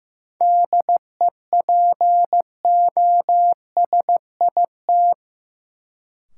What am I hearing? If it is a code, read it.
Morse code, heard as a keyed tone, about 16 wpm, 700 Hz. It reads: DEPOSIT
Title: Sentences from Top 1300 Words - New Word Drill - 15wpm